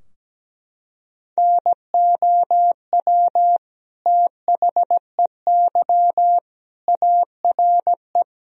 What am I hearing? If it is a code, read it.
Morse code, heard as a keyed tone, about 17 wpm, 700 Hz. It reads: NOW THEY ARE
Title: Sentences from Top 1000 Words - Review Part 2 - 17wpm